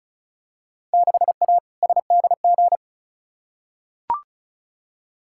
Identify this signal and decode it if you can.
Morse code, heard as a keyed tone, about 34 wpm, 700 Hz. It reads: 6A SDG
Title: Field Day Exchanges Encore - 35wpm